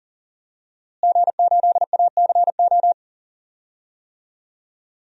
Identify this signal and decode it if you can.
Morse code, heard as a keyed tone, about 38 wpm, 700 Hz. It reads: G8ACO
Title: DXCC-100 Call Signs - 5 Characters - Repeated 3x - 40wpm